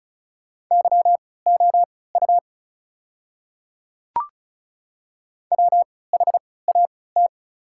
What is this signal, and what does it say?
Morse code, heard as a keyed tone, about 36 wpm, spaced as for 20 wpm, 700 Hz. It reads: YOU WHAT
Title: 3x Character Spacing - Top 100 Words - 35wpm